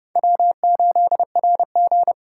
Morse code, heard as a keyed tone, about 30 wpm, 700 Hz. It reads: W8RG
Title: USA Call Signs - 4 Characters - 4x Character Spacing - 30wpm